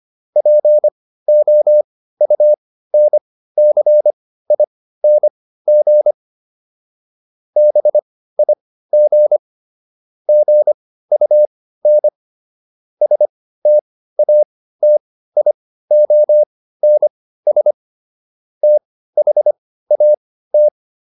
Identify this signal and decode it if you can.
Morse code, heard as a keyed tone, about 25 wpm, spaced as for 15 wpm, 600 Hz. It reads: POUNCING BIG GUN STATIONS THAT